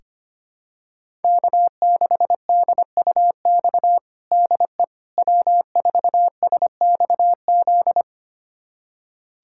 Morse code, heard as a keyed tone, about 25 wpm, 700 Hz. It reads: K6DUX DE W4SXZ
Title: USA Callsign DE Callsign - 5 Characters - 25wpm